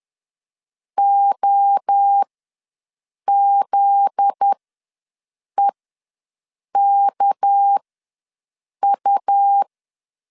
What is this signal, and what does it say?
Morse code, heard as a keyed tone, about 10 wpm, spaced as for 6 wpm, 800 Hz. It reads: OZEKU
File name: LEZIONI / 8 YJ45 / 8 YJ45 ws 10wpm 3843